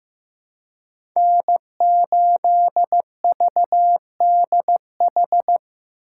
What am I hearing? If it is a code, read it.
Morse code, heard as a keyed tone, about 15 wpm, 700 Hz. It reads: N8VDH
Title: USA Call Signs - 5 Characters - 2x Character Spacing - 15wpm